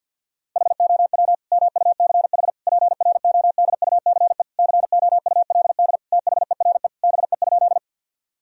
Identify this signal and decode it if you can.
Morse code, heard as a keyed tone, about 50 wpm, 700 Hz. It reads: HOW MUCH PRODUCE COULD THERE BE?
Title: Sentences from Top 1000 Words - Review Part 3 - 50wpm